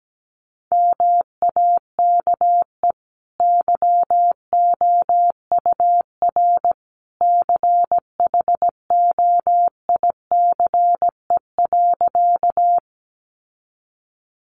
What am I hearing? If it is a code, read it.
Morse code, heard as a keyed tone, about 17 wpm, 700 Hz. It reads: MAKE YOUR CHOICE.
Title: Sentences from Top 900 Words 17wpm